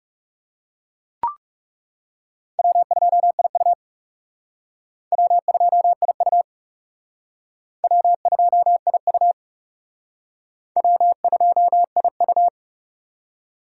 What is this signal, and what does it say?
Morse code, heard as a keyed tone, about 35 wpm, 690 Hz. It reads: W2IU W2IU W2IU W2IU
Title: Speed Racing - USA Call Signs 30wpm